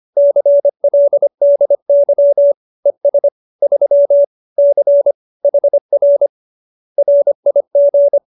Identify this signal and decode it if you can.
Morse code, heard as a keyed tone, about 25 wpm, 570 Hz. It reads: CLDY ES 3 C HR RIG